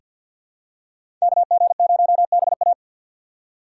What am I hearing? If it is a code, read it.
Morse code, heard as a keyed tone, about 50 wpm, 700 Hz. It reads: KG0BA
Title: USA Call Signs - 5 Characters - 6x Character Spacing - 50wpm